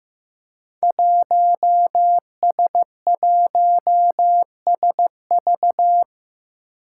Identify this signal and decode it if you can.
Morse code, heard as a keyed tone, about 15 wpm, 700 Hz. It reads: 1S1SV
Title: Five Challenging Letter-Number Combinations 15wpm